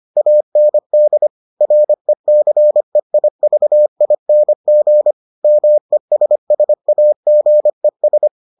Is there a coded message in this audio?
Yes: Morse code from 0.2 to 8.3 s, heard as a keyed tone, about 25 wpm, 600 Hz. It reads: AND RECEIVING MESSAGES